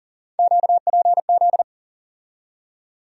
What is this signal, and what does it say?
Morse code, heard as a keyed tone, about 40 wpm, 700 Hz. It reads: QPZ
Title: USA Call Sign Postfixes - 3 Letters 40wpm